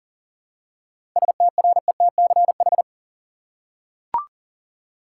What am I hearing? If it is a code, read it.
Morse code, heard as a keyed tone, about 37 wpm, 700 Hz. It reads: STRETCH
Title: Seven Letter Words - 40wpm